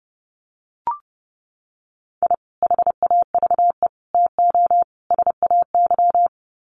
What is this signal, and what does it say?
Morse code, heard as a keyed tone, about 30 wpm, 700 Hz. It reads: I HAVE TO SAY
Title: Sets of 4 Words 30wpm